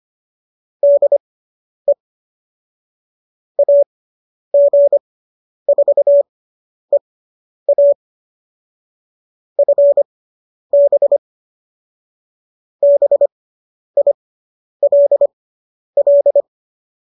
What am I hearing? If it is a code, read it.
Morse code, heard as a keyed tone, about 25 wpm, spaced as for 10 wpm, 570 Hz. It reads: DE AG4EA FB BILL